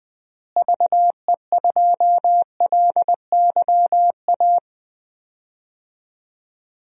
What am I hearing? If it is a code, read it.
Morse code, heard as a keyed tone, about 20 wpm, 700 Hz. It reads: VE2LYA
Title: DXCC-100 Call Signs - 6 Characters - Repeated 2x - 20wpm